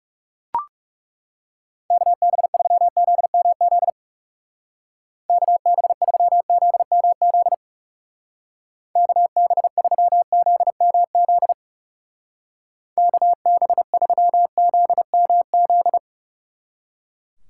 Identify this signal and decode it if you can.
Morse code, heard as a keyed tone, about 38 wpm, 700 Hz. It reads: KB3ZMZ KB3ZMZ KB3ZMZ KB3ZMZ